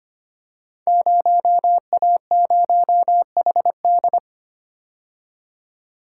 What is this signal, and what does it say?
Morse code, heard as a keyed tone, about 25 wpm, 700 Hz. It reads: 0A0HD
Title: Code Groups - 5 Letter:Numbers - 25wpm